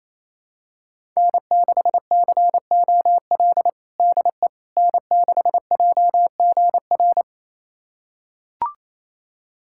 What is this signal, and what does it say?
Morse code, heard as a keyed tone, about 27 wpm, 700 Hz. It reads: N6COL DE N6JGR
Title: USA Callsign DE Callsign - 5 Characters - 28wpm